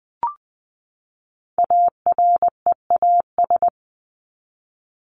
Morse code, heard as a keyed tone, about 20 wpm, 700 Hz. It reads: AREAS
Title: ES:S Suffix - Top 100 Words with Base Words 20wpm